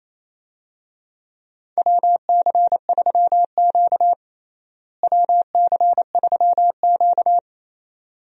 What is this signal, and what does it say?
Morse code, heard as a keyed tone, about 28 wpm, 700 Hz. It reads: WC3Q WC3Q
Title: USA Call Signs - 4 Characters - Repeated 2x - 28wpm